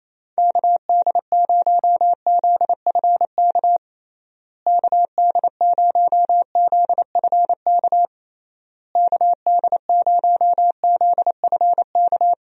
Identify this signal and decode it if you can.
Morse code, heard as a keyed tone, about 28 wpm, 700 Hz. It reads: KD0ZFK KD0ZFK KD0ZFK
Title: USA Call Signs - 6 Characters - Repeated 3x - 28wpm